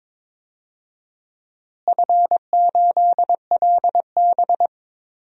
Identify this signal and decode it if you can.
Morse code, heard as a keyed tone, about 22 wpm, 700 Hz. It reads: F8LB
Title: DXCC-100 Call Signs - 4 Characters - Repeated 3x - 22wpm